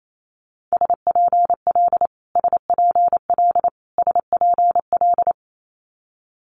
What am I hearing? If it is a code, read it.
Morse code, heard as a keyed tone, about 28 wpm, 700 Hz. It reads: SPL SPL SPL
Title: Multi-letter Phonemes Repeated 3x 28wpm